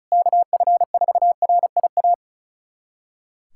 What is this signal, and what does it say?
Morse code, heard as a keyed tone, about 34 wpm, 700 Hz. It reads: KF4RIA
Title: USA Callsign DE Callsign 35wpm